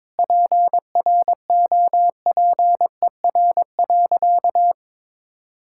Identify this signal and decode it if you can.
Morse code, heard as a keyed tone, about 22 wpm, 700 Hz. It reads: PROPER.
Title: Sentences from Top 1000 Words - Review Part 5 - 22wpm